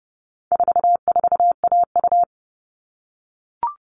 Morse code, heard as a keyed tone, about 30 wpm, 700 Hz. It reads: 44AU
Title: All Cut Numbers - 4-Digit 30wpm